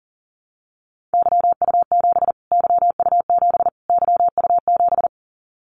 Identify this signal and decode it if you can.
Morse code, heard as a keyed tone, about 40 wpm, 700 Hz. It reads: YU7 YU7 YU7
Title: Callsign Prefixes for Top 25 DXCC - Repeated 3x - 40wpm